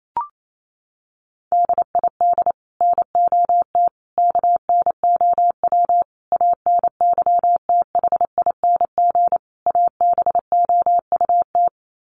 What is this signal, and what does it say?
Morse code, heard as a keyed tone, about 28 wpm, 700 Hz. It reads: DID NOT KNOW ANYTHING ABOUT
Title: Sets of 5 Words Encore2 28wpm